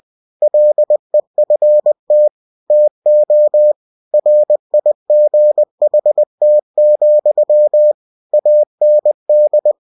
Morse code, heard as a keyed tone, about 20 wpm, 600 Hz. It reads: LEFT TO RIGHT, AND
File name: cw-books / winnie-the-pooh / winnie-20wpm-page-0029